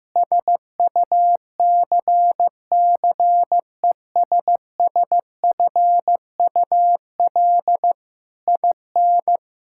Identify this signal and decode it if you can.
Morse code, heard as a keyed tone, about 14 wpm, 700 Hz. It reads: SUCCESSFUL IN
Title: Sentences from Top 1000 Words - Review Part 6 - 15wpm